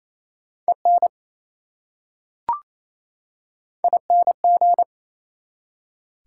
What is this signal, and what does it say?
Morse code, heard as a keyed tone, about 28 wpm, 700 Hz. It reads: EN ING
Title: Prefixes and Suffixes 28wpm